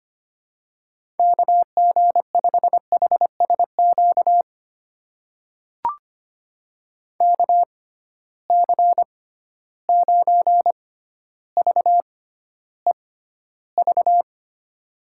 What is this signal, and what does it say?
Morse code, heard as a keyed tone, about 25 wpm, spaced as for 13 wpm, 700 Hz. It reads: KG5HSQ KC9VEV
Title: USA Call Signs - 6 Characters - 6x Character Spacing - 25wpm